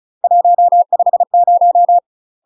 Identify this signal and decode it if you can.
Morse code, heard as a keyed tone, about 35 wpm, 700 Hz. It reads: 150
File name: UK MASTER QSOs TRAINING / 35WPM / UK MASTER QSOs 0019